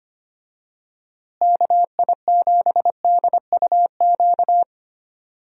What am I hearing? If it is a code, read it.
Morse code, heard as a keyed tone, about 25 wpm, 700 Hz. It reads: KI7DUQ
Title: USA Call Signs - 6 Characters - Repeated 3x - 25wpm